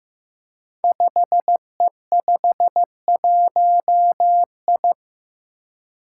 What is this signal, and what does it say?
Morse code, heard as a keyed tone, about 15 wpm, 700 Hz. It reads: 5E51I
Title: Five Challenging Letter-Number Combinations 15wpm